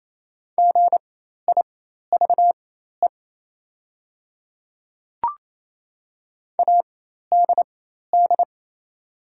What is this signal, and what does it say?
Morse code, heard as a keyed tone, about 28 wpm, spaced as for 13 wpm, 700 Hz. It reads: GIVE ADD
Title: 4x Character Spacing - Top 300 Words - 28wpm